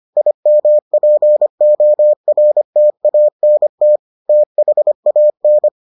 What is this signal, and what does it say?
Morse code, heard as a keyed tone, about 25 wpm, 600 Hz. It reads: IMPORTANT THAN